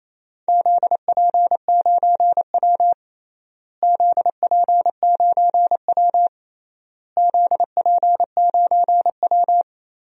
Morse code, heard as a keyed tone, about 28 wpm, 700 Hz. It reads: ZP9W ZP9W ZP9W
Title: DXCC-100 Call Signs - 4 Characters - Repeated 3x - 28wpm